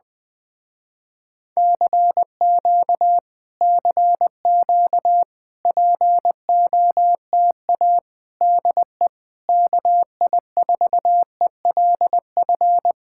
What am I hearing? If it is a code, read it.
Morse code, heard as a keyed tone, about 20 wpm, 700 Hz. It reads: CQ CQ POTA DE KI4ELF
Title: Speed-Racing Common QSO Phrases Encore - 20wpm